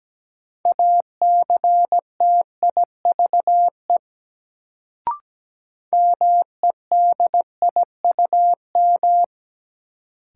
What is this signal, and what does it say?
Morse code, heard as a keyed tone, about 17 wpm, 700 Hz. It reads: ACTIVE MEDIUM
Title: Sentences from Top 900 Words - New Word Drill - 17wpm